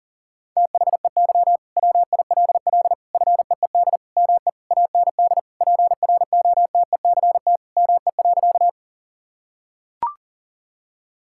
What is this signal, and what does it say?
Morse code, heard as a keyed tone, about 37 wpm, 700 Hz. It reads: THEY WILL FEED ME AND PROTECT ME.
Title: Sentences from Top 1600 Words 40wpm